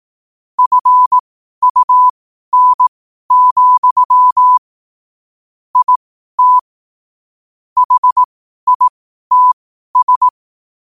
Morse code, heard as a keyed tone, about 18 wpm, spaced as for 12 wpm, 1000 Hz. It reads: FUN, IT HITS
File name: qotdfiles / 12wpm